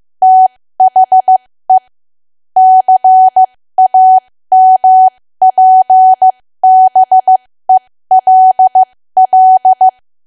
Morse code, heard as a keyed tone, about 15 wpm, spaced as for 13 wpm, 750 Hz. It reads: THE CAMPBELL
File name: cwa / W1AW-Level2 / 13WPM